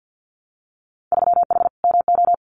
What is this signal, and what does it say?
Morse code, heard as a keyed tone, about 50 wpm, 700 Hz. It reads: 3H MO
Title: Winter Field Day Exchanges Encore 50wpm